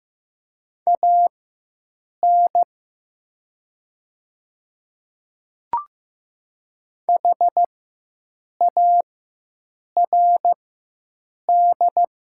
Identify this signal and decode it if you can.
Morse code, heard as a keyed tone, about 15 wpm, spaced as for 7 wpm, 700 Hz. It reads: AN HARD
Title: 4x Character Spacing - Top 300 Words - 15wpm